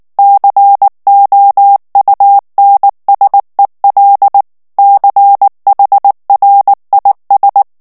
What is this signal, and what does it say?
Morse code, heard as a keyed tone, about 19 wpm, 800 Hz. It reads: COUNSEL CHRIS
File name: ANGLAIS texte clair 3 vitesse 18 mots minute